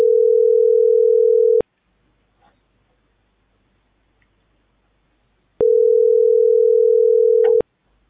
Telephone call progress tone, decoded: ringback tone

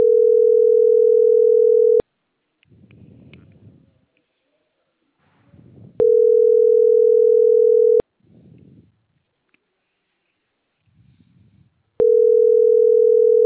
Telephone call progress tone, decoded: ringback tone

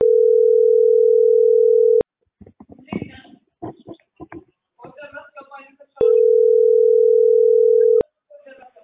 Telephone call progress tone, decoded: ringback tone

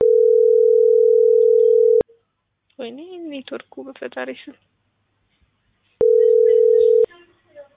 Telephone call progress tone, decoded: ringback tone

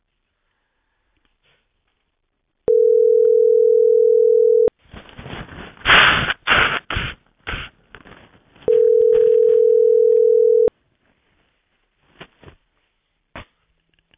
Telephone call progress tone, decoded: ringback tone

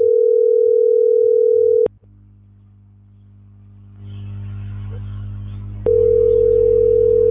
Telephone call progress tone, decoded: ringback tone

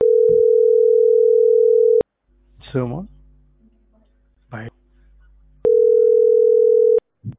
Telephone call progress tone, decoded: ringback tone

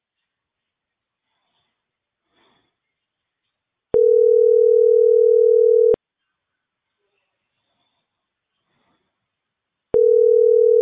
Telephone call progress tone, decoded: ringback tone

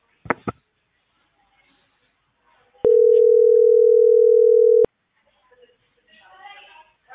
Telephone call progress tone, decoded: ringback tone

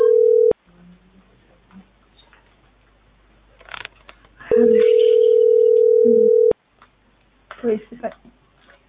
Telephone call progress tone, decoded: ringback tone

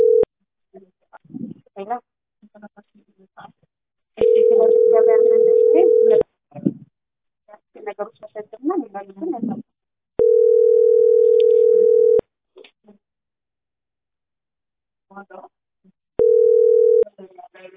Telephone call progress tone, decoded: ringback tone

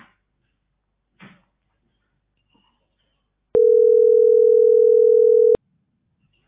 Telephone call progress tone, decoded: ringback tone

